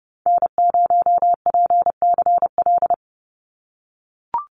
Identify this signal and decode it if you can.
Morse code, heard as a keyed tone, about 28 wpm, 700 Hz. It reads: N0PCL
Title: NA SOTA Activator Callsigns 30wpm